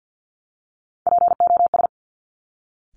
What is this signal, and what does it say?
Morse code, heard as a keyed tone, about 50 wpm, 700 Hz. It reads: POS
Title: Top Three Letter Combinations in Words 50wpm